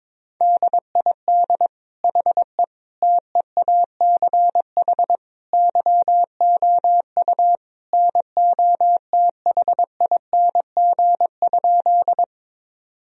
Morse code, heard as a keyed tone, about 22 wpm, 700 Hz. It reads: DID HE TEACH YOU NOTHING?
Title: Sentences from Top 1000 Words - Review Part 2 - 22wpm